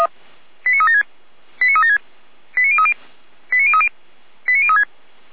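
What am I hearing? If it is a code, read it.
Five-tone selective call (ZVEI), heard as quick runs of five tones, tones of about 70 ms. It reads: 89376 89376 89030 89030 89036
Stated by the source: ZVEI five-tone selective call received on an SDR, fire brigade feuerwehr-orlamuende.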